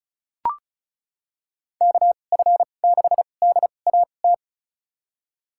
Morse code, heard as a keyed tone, about 36 wpm, spaced as for 25 wpm, 700 Hz. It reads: KF6DAT
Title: USA Call Signs - 6 Characters - 2x Character Spacing - 35wpm